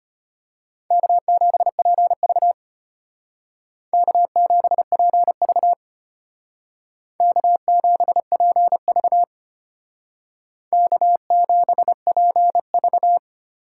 Morse code, heard as a keyed tone, about 32 wpm, 700 Hz. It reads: K7PV K7PV K7PV K7PV